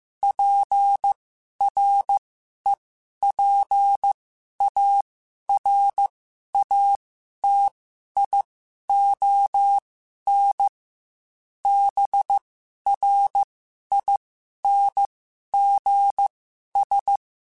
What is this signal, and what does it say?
Morse code, heard as a keyed tone, about 15 wpm, spaced as for 11 wpm, 800 Hz. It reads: PREPARATION BRINGS
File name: Quotes / Quotes-mp3s / Persperation @13